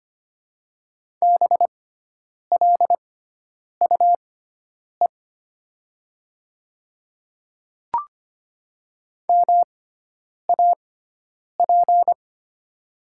Morse code, heard as a keyed tone, about 25 wpm, spaced as for 9 wpm, 700 Hz. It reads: BLUE MAP
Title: 6x Character Spacing - Top 500 Words - 25wpm